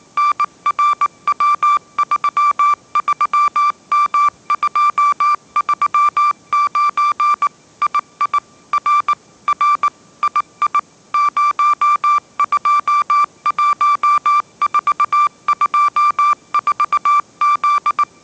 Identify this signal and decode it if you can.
Morse code, heard as a keyed tone, about 19 wpm, 1170 Hz. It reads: NRW33M239 II R R II 021424Z
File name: mrx083 20wpm